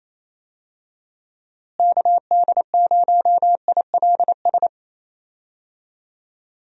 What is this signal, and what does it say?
Morse code, heard as a keyed tone, about 27 wpm, 700 Hz. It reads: KD0ILS
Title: USA Call Signs - 6 Characters - Repeated 3x - 28wpm